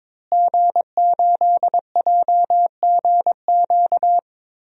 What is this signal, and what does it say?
Morse code, heard as a keyed tone, about 21 wpm, 700 Hz. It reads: G8JGQ
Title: DXCC-100 Call Signs - 5 Characters - Repeated 2x - 22wpm